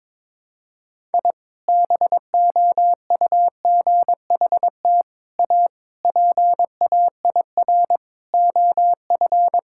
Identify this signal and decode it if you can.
Morse code, heard as a keyed tone, about 22 wpm, 700 Hz. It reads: I BOUGHT A PAIR OF
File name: practice / 22wpm / Sentences with 3-Word Phrases - 22wpm